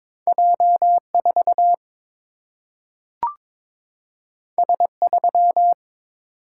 Morse code, heard as a keyed tone, about 22 wpm, 700 Hz. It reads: J4 S3
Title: Single-Letter Single-Number 22wpm